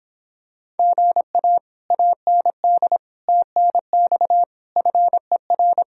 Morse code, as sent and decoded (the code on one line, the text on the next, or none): --. .- / .- -. -.. / - -. -..- / ..-. .-.
GA AND TNX FER